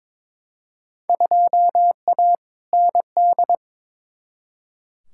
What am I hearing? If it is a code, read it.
Morse code, heard as a keyed tone, about 21 wpm, 700 Hz. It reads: 2A ND
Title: Field Day Exchanges Encore - 22wpm